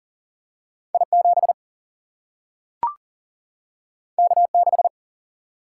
Morse code, heard as a keyed tone, about 40 wpm, 700 Hz. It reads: I 7 K 6
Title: Single-Letter Single-Number 40wpm